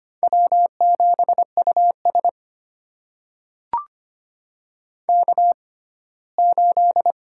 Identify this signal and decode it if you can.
Morse code, heard as a keyed tone, about 25 wpm, spaced as for 17 wpm, 700 Hz. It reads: W7US K8
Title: USA Call Signs - 4 Characters - 6x Character Spacing - 25wpm